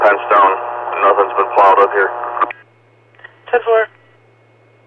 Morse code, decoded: none

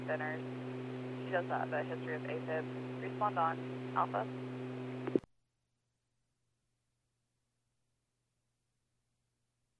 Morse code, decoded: none

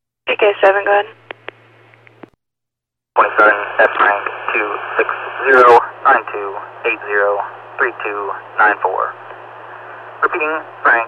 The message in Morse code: none

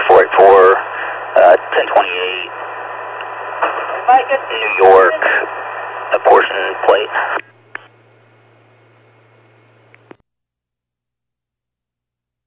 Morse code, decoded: none

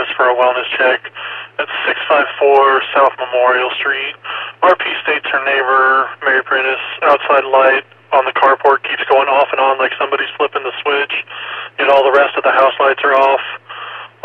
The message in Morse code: none

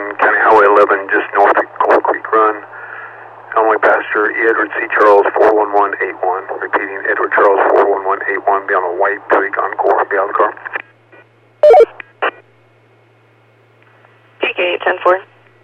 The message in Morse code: none

